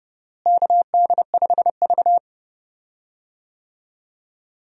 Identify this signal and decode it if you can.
Morse code, heard as a keyed tone, about 29 wpm, 700 Hz. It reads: KD5V
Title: USA Call Signs - 4 Characters - Repeated 3x - 30wpm